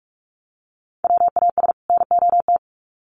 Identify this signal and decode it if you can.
Morse code, heard as a keyed tone, about 44 wpm, 700 Hz. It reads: WAS NOT
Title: Sets of 2 Words 45wpm